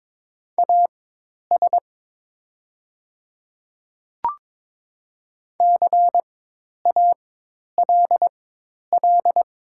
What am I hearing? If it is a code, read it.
Morse code, heard as a keyed tone, about 22 wpm, spaced as for 10 wpm, 700 Hz. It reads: AS CALL
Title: 4x Character Spacing - Top 300 Words - 22wpm